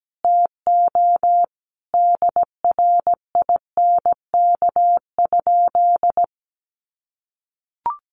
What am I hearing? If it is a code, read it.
Morse code, heard as a keyed tone, about 17 wpm, 700 Hz. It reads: TO DRINK?
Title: Sentences from Top 1500 Words 17wpm